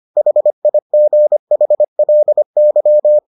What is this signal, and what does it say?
Morse code, heard as a keyed tone, about 25 wpm, 600 Hz. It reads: HIGHLY